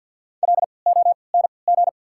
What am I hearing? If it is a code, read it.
Morse code, heard as a keyed tone, about 54 wpm, spaced as for 29 wpm, 690 Hz. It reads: LONG
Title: 3x Character Spacing - Top 300 Words - 50wpm